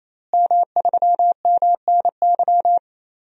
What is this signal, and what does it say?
Morse code, heard as a keyed tone, about 28 wpm, 700 Hz. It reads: M3MNY
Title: DXCC-100 Call Signs - 5 Characters - Repeated 3x - 28wpm